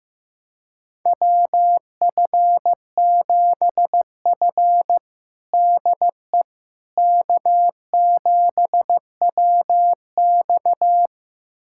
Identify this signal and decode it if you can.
Morse code, heard as a keyed tone, about 15 wpm, 700 Hz. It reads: WF7F DE K7WX
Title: USA Callsign DE Callsign - 4 Characters - 15wpm